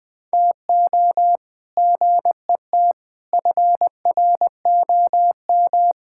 Morse code, heard as a keyed tone, about 20 wpm, 700 Hz. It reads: TO GET FROM
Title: Sentences with 8-Word Phrases - 20wpm